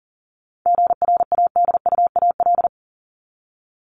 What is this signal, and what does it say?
Morse code, heard as a keyed tone, about 40 wpm, 700 Hz. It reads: GRADUAL
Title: LY Suffix - Top 100 Words with Base Words 40wpm